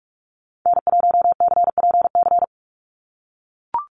Morse code, heard as a keyed tone, about 44 wpm, 700 Hz. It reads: N1CPC
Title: USA Call Signs 45wpm